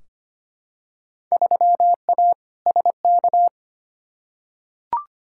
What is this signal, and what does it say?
Morse code, heard as a keyed tone, about 25 wpm, 700 Hz. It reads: 3A SK
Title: Field Day Exchanges Encore - 25wpm